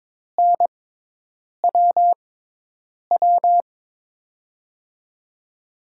Morse code, heard as a keyed tone, about 22 wpm, spaced as for 8 wpm, 700 Hz. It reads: NWW
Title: USA Call Signs - 6 Characters - 6x Character Spacing - 22wpm